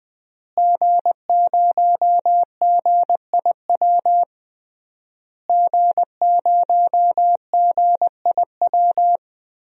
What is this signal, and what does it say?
Morse code, heard as a keyed tone, about 21 wpm, 700 Hz. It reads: G0GIW G0GIW
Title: DXCC-100 Call Signs - 5 Characters - Repeated 3x - 20wpm